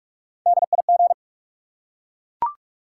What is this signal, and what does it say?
Morse code, heard as a keyed tone, about 44 wpm, 700 Hz. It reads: DIG
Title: Three Letter Words - 45wpm